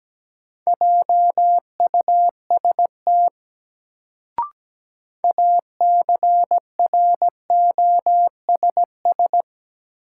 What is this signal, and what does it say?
Morse code, heard as a keyed tone, about 17 wpm, 700 Hz. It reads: JUST ACROSS
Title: Sets of 2 Words - New Word Drill - 17wpm